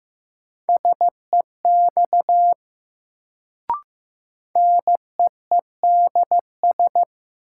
Morse code, heard as a keyed tone, about 15 wpm, 700 Hz. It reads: SEX NEEDS
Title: Sentences from Top 1600 Words - New Word Drill Alt - 15wpm